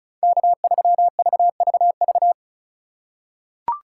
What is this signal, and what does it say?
Morse code, heard as a keyed tone, about 36 wpm, 700 Hz. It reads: K3VVV
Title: USA Call Signs - 5 Characters - 2x Character Spacing - 35wpm